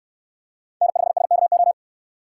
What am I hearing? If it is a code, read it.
Morse code, heard as a keyed tone, about 66 wpm, 700 Hz. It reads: N5SKO